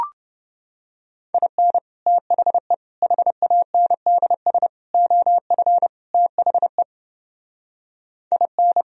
Morse code, heard as a keyed tone, about 29 wpm, 700 Hz. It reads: IN THE HANDS OF THE IN